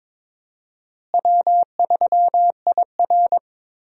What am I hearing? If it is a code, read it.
Morse code, heard as a keyed tone, about 21 wpm, 700 Hz. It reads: W3IR
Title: USA Call Signs - 4 Characters - Repeated 3x - 22wpm